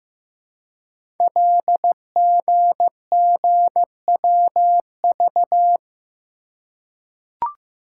Morse code, heard as a keyed tone, about 15 wpm, 700 Hz. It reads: LGGWV
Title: Code Groups - 5 Letters - 15wpm